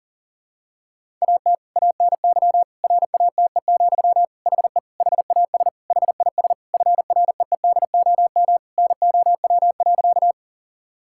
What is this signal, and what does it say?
Morse code, heard as a keyed tone, about 40 wpm, 700 Hz. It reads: AT ANY RATE, HE HAS HIS FREEDOM NOW.
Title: Sentences from Top 1000 Words - Review Part 5 - 40wpm